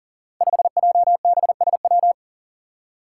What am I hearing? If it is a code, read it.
Morse code, heard as a keyed tone, about 40 wpm, 700 Hz. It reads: 5JBSW
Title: Five Challenging Letter-Number Combinations 40wpm